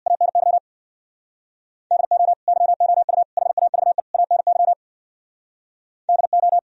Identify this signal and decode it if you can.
Morse code, heard as a keyed tone, about 64 wpm, 700 Hz. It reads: ANY DO YOU HAVE ANY DO